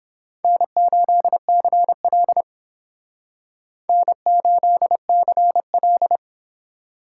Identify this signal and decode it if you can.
Morse code, heard as a keyed tone, about 26 wpm, 700 Hz. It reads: N8CL N8CL
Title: Speed Racing - USA Call Signs 22wpm